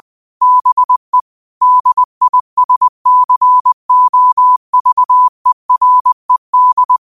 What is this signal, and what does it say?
Morse code, heard as a keyed tone, about 20 wpm, 1000 Hz. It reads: BE DISCOVERED
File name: qotdfiles / 20wpm